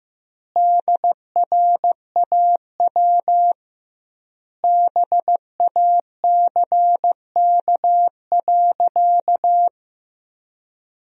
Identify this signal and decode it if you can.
Morse code, heard as a keyed tone, about 15 wpm, 700 Hz. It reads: DRAW BACK.
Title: 2x Word-Spacing - Sentences from Top 300 Words - 15wpm